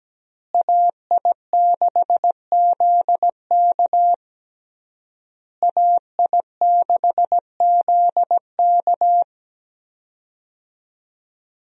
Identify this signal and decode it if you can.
Morse code, heard as a keyed tone, about 17 wpm, 700 Hz. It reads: AI6ZK AI6ZK
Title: USA Call Signs - 5 Characters - Repeated 3x - 17wpm